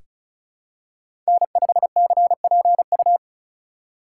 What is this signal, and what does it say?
Morse code, heard as a keyed tone, about 36 wpm, 700 Hz. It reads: N5CPU
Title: USA Call Signs - 5 Characters - 3x Character Spacing - 35wpm